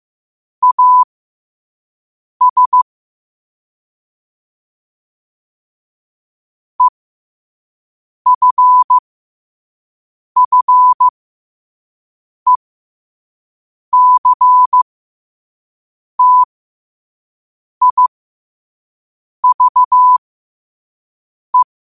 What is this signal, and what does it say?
Morse code, heard as a keyed tone, about 15 wpm, spaced as for 5 wpm, 1000 Hz. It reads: AS EFFECTIVE
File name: qotdfiles / 5wpm